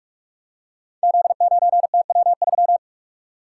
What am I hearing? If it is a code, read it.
Morse code, heard as a keyed tone, about 44 wpm, 700 Hz. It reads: Z9TW3
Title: Code Groups - 5 Letter:Numbers - 45wpm